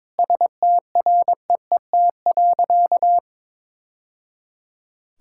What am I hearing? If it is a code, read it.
Morse code, heard as a keyed tone, about 22 wpm, 700 Hz. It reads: STREET.
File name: practice / 22wpm / Sentences with 3-Word Phrases - 22wpm